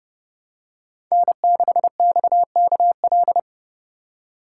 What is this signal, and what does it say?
Morse code, heard as a keyed tone, about 29 wpm, 700 Hz. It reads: N6XKL